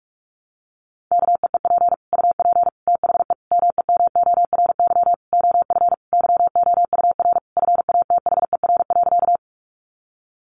Morse code, heard as a keyed tone, about 46 wpm, 700 Hz. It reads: KEEP UP THE MEMORY OF YOUR FATHER.
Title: Sentences from Top 1300 Words 45wpm